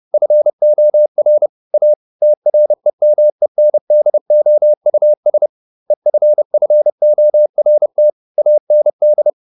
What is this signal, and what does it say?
Morse code, heard as a keyed tone, about 30 wpm, 600 Hz. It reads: FOR A TREMENDOUS EFFORT AND